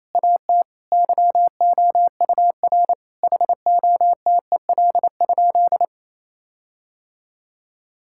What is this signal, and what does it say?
Morse code, heard as a keyed tone, about 28 wpm, 700 Hz. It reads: AT YOUR HOTEL?
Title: Sentences from Top 1600 Words 28wpm